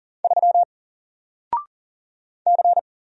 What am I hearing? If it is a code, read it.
Morse code, heard as a keyed tone, about 39 wpm, 700 Hz. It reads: 3 C
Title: Single Letter-Number 40wpm